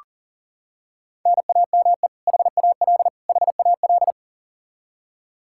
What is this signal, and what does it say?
Morse code, heard as a keyed tone, about 37 wpm, 700 Hz. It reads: NAME HAL HAL
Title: Practice QSO - Exchange 1 - Beginning - 40wpm